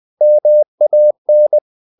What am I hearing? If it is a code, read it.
Morse code, heard as a keyed tone, about 20 wpm, 600 Hz. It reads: MAN
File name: the old man and the sea / the old man and the sea part 1 20wpm